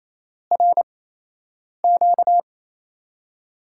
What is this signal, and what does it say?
Morse code, heard as a keyed tone, about 28 wpm, 700 Hz. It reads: R Q
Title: USA Call Signs - 6 Characters - 8x Character Spacing - 28wpm